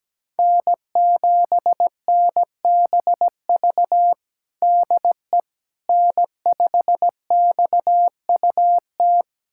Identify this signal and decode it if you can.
Morse code, heard as a keyed tone, about 17 wpm, 700 Hz. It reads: N7NBV DE N5XUT
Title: USA Callsign DE Callsign - 5 Characters - 17wpm